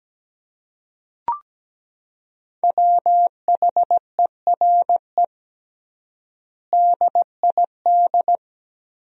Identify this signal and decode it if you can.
Morse code, heard as a keyed tone, about 17 wpm, 700 Hz. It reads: WHERE DID
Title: 3x Word-Spacing - Sentences from Top 200 Words - 17wpm